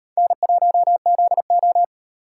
Morse code, heard as a keyed tone, about 38 wpm, 700 Hz. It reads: N1ZO